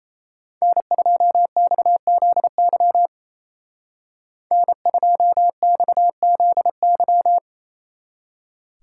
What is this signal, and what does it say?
Morse code, heard as a keyed tone, about 35 wpm, 700 Hz. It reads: N2XZY N2XZY
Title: Speed Racing - USA 1x3 Call Signs 28wpm